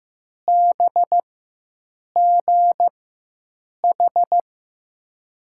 Morse code, heard as a keyed tone, about 16 wpm, spaced as for 7 wpm, 700 Hz. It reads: BGH
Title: USA Call Signs - 6 Characters - 4x Character Spacing - 15wpm